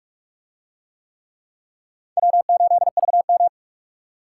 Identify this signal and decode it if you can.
Morse code, heard as a keyed tone, about 44 wpm, 700 Hz. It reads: W8VM